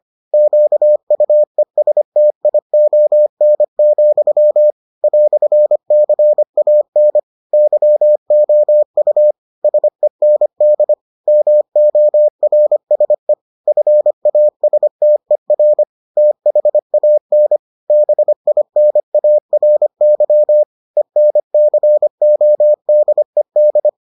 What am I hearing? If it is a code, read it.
Morse code, heard as a keyed tone, about 25 wpm, 600 Hz. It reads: QUESTION, "CAN YOU SEND MORSE FASTER THAN BINARY ENCODED